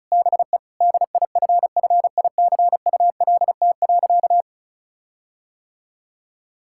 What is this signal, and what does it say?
Morse code, heard as a keyed tone, about 34 wpm, 700 Hz. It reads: BE DIFFICULT.
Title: Sentences from Top 1600 Words 35wpm